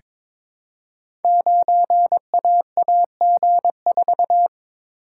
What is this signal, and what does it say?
Morse code, heard as a keyed tone, about 22 wpm, 700 Hz. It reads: 9AAG4
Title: Code Groups - 5 Letter:Numbers - 22wpm